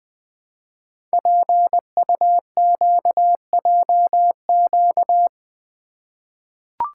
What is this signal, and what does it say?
Morse code, heard as a keyed tone, about 20 wpm, 700 Hz. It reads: PUQJQ
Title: Code Groups - 5 Letters - 20wpm